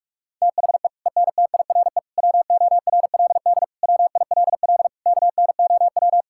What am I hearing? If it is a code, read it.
Morse code, heard as a keyed tone, about 46 wpm, 690 Hz. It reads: THE ENTIRE WORLD WILL KNOW
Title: Sentences from Top 1000 Words - Review Part 5 - 45wpm